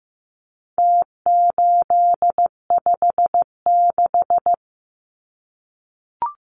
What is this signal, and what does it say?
Morse code, heard as a keyed tone, about 15 wpm, 700 Hz. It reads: T856
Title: Numbers - Contest Style - 4-Digit 15wpm